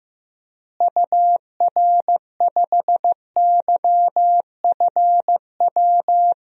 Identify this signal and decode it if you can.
Morse code, heard as a keyed tone, about 15 wpm, 700 Hz. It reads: UR5YFW
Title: DXCC-100 Call Signs - 6 Characters - Repeated 2x - 15wpm